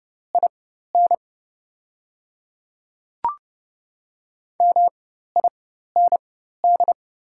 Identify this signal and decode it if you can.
Morse code, heard as a keyed tone, about 29 wpm, spaced as for 14 wpm, 700 Hz. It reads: IN MIND
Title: 4x Character Spacing - Top 500 Words - 30wpm